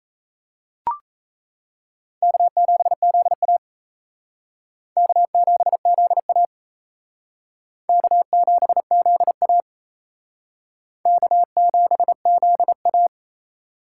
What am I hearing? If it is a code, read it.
Morse code, heard as a keyed tone, about 35 wpm, 700 Hz. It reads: K7ZA K7ZA K7ZA K7ZA